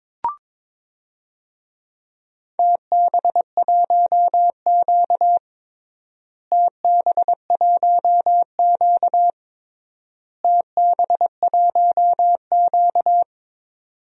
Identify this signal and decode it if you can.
Morse code, heard as a keyed tone, about 22 wpm, 700 Hz. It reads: TB1Q TB1Q TB1Q
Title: DXCC-100 Call Signs - 4 Characters - Repeated 3x - 22wpm